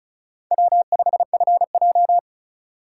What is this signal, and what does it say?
Morse code, heard as a keyed tone, about 43 wpm, 700 Hz. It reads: W5FJ
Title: Speed Racing - USA 1x2 Call Signs 35wpm